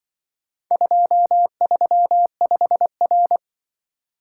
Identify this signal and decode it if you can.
Morse code, heard as a keyed tone, about 24 wpm, 700 Hz. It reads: 235R
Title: Speed-Racing - Four Letter-Number Combinations - 20wpm